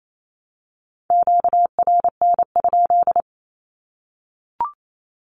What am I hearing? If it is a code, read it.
Morse code, heard as a keyed tone, about 28 wpm, 700 Hz. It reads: QRN?
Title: Q-Codes Repeated 3x 28wpm